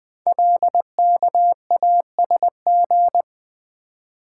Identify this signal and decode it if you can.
Morse code, heard as a keyed tone, about 20 wpm, 700 Hz. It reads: LKASG
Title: Code Groups - 5 Letters - 20wpm